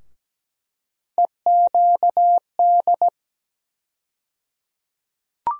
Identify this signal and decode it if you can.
Morse code, heard as a keyed tone, about 17 wpm, 700 Hz. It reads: EQD
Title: USA Call Sign Postfixes - 3 Letters 17wpm